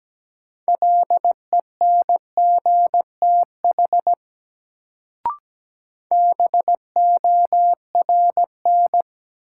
Morse code, heard as a keyed tone, about 17 wpm, 700 Hz. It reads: LENGTH BORN
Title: Sentences from Top 1000 Words - New Word Drill - 17wpm